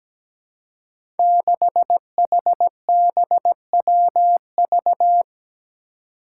Morse code, heard as a keyed tone, about 17 wpm, 700 Hz. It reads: 6HBWV
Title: Five Challenging Letter-Number Combinations 17wpm